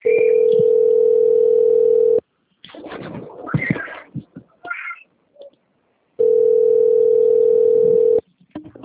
Ringback tone (North America), 440 + 480 Hz.